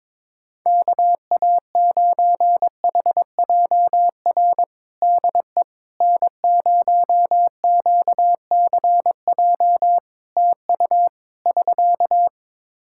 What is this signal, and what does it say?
Morse code, heard as a keyed tone, about 22 wpm, 700 Hz. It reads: KA9HJR DE N0QCJ TU <SK>
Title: Practice QSO - Exchange 4 - Ending - 22wpm